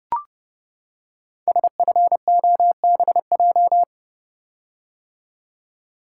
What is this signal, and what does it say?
Morse code, heard as a keyed tone, about 28 wpm, 700 Hz. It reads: SFOBJ
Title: Code Groups - 5 Letters - 30wpm